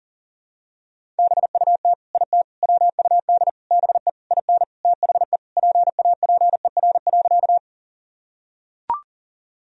Morse code, heard as a keyed tone, about 40 wpm, 700 Hz. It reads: BUT IT WUD BE IN THE PAPER.
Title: Sentences from Top 300 Words and Abbreviations 40wpm